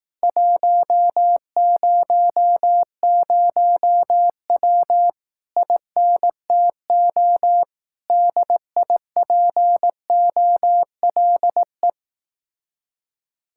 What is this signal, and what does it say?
Morse code, heard as a keyed tone, about 18 wpm, 700 Hz. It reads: 100W INTO DIPOLE
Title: Speed-Racing Common QSO Phrases Encore - 15wpm